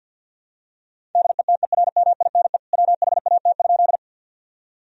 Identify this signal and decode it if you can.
Morse code, heard as a keyed tone, about 50 wpm, 700 Hz. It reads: DETERMINE WHAT?
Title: Sentences from Top 1000 Words - Review Part 4 - 50wpm